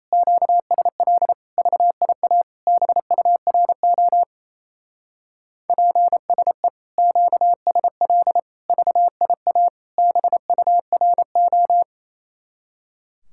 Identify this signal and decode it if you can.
Morse code, heard as a keyed tone, about 36 wpm, 700 Hz. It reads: QSL VIA BURO PSE QSL VIA BURO